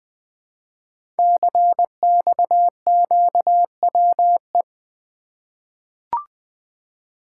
Morse code, heard as a keyed tone, about 20 wpm, 700 Hz. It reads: CXQWE
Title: Code Groups - 5 Letters - 20wpm